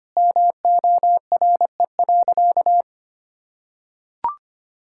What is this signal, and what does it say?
Morse code, heard as a keyed tone, about 25 wpm, 700 Hz. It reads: MORE.
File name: practice / 25wpm / Sentences with 2-Word Phrases - 25wpm